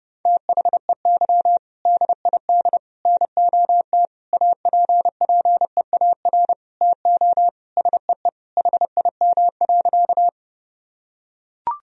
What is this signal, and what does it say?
Morse code, heard as a keyed tone, about 30 wpm, 700 Hz. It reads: THEY DID NOT APPEAR TO SEE HIM.
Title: Sentences from Top 1000 Words - Review Part 2 - 30wpm